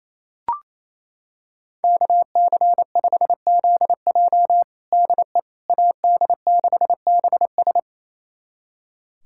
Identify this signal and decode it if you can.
Morse code, heard as a keyed tone, about 28 wpm, 700 Hz. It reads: KC5ZJ DE AD6BS
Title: USA Callsign DE Callsign - 5 Characters - 28wpm